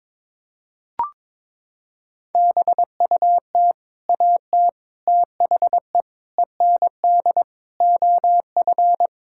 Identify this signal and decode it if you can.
Morse code, heard as a keyed tone, about 22 wpm, 700 Hz. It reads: BUT AT THE END OF
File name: practice / 22wpm / Sets of 8 Words - 22wpm